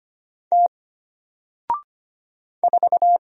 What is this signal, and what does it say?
Morse code, heard as a keyed tone, about 25 wpm, 700 Hz. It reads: T 4
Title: Single Character (Letter:Number:Punctuation:Prosign) Alt 25wpm